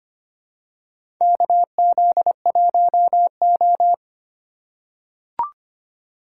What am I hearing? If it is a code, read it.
Morse code, heard as a keyed tone, about 25 wpm, 690 Hz. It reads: KZ1O
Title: USA Call Signs - 4 Characters - 4x Character Spacing - 25wpm